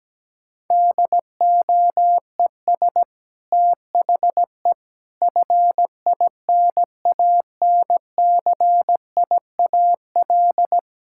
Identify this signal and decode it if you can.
Morse code, heard as a keyed tone, about 17 wpm, 700 Hz. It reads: DOES THE FINANCIAL
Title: Sentences from Top 1000 Words - Review Part 5 - 17wpm